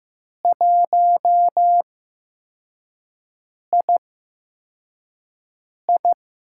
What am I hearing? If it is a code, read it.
Morse code, heard as a keyed tone, about 15 wpm, spaced as for 4 wpm, 700 Hz. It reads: 1II
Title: USA Call Signs - 5 Characters - 8x Character Spacing - 15wpm